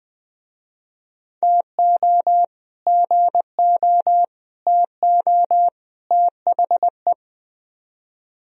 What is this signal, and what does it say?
Morse code, heard as a keyed tone, about 21 wpm, 700 Hz. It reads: TO GO TO THE
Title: Speed-Racing - Sets of 4 Words 17wpm